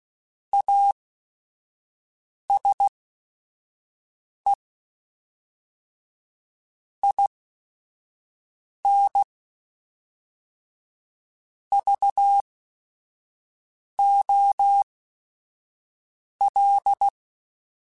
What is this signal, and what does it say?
Morse code, heard as a keyed tone, about 16 wpm, spaced as for 5 wpm, 800 Hz. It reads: ASE IN VOL
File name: Technical Subjects / Technical Subjects-mp3s / XC @5